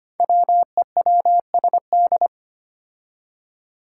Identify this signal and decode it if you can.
Morse code, heard as a keyed tone, about 25 wpm, 700 Hz. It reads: WEWSD
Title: Five Challenging Letter-Number Combinations 25wpm